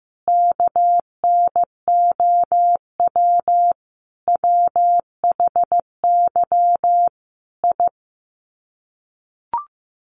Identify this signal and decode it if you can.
Morse code, heard as a keyed tone, about 15 wpm, 700 Hz. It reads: KNOW WHY I
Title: Sets of 5 Words Encore 15wpm